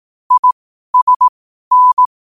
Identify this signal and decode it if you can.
Morse code, heard as a keyed tone, about 18 wpm, spaced as for 12 wpm, 1000 Hz. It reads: ISN
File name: qotdfiles / 12wpm